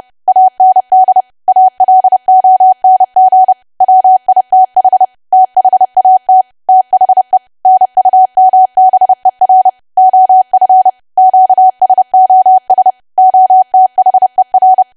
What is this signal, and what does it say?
Morse code, heard as a keyed tone, about 30 wpm, 750 Hz. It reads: AND ALONG WITH THAT THE NUMBER OF QSOS OTHER
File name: cwa / W1AW-Level3 / 30WPM